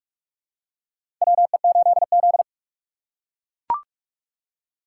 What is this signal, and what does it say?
Morse code, heard as a keyed tone, about 43 wpm, 700 Hz. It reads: WE8Z
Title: USA Call Signs - 4 Characters - 6x Character Spacing - 45wpm